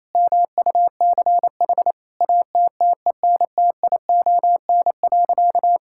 Morse code, heard as a keyed tone, about 28 wpm, 700 Hz. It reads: MUCH ATTENTION.
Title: Sentences from Top 1000 Words - Review Part 6 - 28wpm